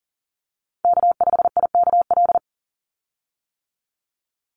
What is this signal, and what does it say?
Morse code, heard as a keyed tone, about 40 wpm, 700 Hz. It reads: K5IKL
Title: USA Call Signs 40wpm